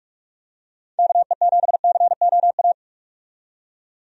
Morse code, heard as a keyed tone, about 43 wpm, 700 Hz. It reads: KE7COA
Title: USA Call Signs - 6 Characters - 45wpm